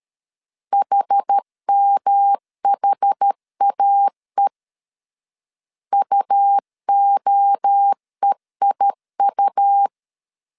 Morse code, heard as a keyed tone, about 12 wpm, 800 Hz. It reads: HMHAE UOEIU
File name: CW / DET / 1 3 1500C S B 12wpm 2828